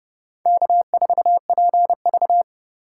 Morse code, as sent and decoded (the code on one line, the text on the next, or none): -.- ....- .--. ...-
K4PV